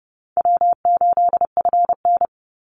Morse code, heard as a keyed tone, about 30 wpm, 700 Hz. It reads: W8FN
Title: Easy Call Signs Repeated 3x - 30wpm